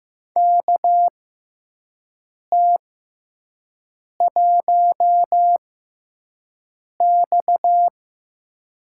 Morse code, heard as a keyed tone, about 15 wpm, spaced as for 5 wpm, 700 Hz. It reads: KT1X
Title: USA Call Signs - 4 Characters - 6x Character Spacing - 15wpm